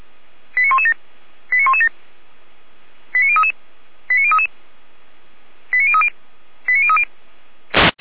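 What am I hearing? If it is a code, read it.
Five-tone selective call (ZVEI), heard as quick runs of five tones, tones of about 70 ms. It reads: 89117 89117 89033 89033 89030 89030
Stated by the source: ZVEI five-tone selective call received by SDR.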